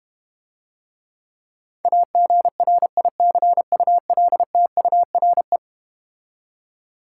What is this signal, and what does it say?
Morse code, heard as a keyed tone, about 31 wpm, 690 Hz. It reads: AGRICULTURE